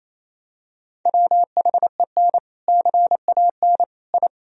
Morse code, heard as a keyed tone, about 28 wpm, 700 Hz. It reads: WHEN CAN I
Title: Sentences from Top 1000 Words - Review Part 3 - 28wpm